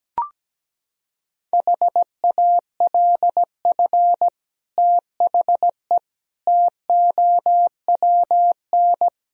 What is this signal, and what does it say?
Morse code, heard as a keyed tone, about 17 wpm, 700 Hz. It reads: HALF THE TOWN
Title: Sentences from Top 1000 Words - Review Part 2 - 17wpm